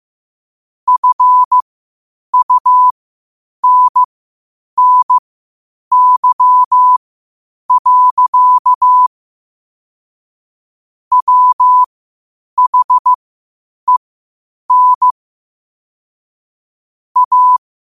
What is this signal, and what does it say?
Morse code, heard as a keyed tone, about 15 wpm, spaced as for 8 wpm, 1000 Hz. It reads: FUNNY. WHEN A
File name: qotdfiles / 8wpm